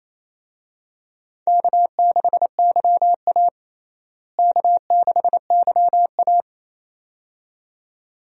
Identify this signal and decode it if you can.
Morse code, heard as a keyed tone, about 28 wpm, 700 Hz. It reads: K6YA K6YA
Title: USA Call Signs - 4 Characters - Repeated 2x - 28wpm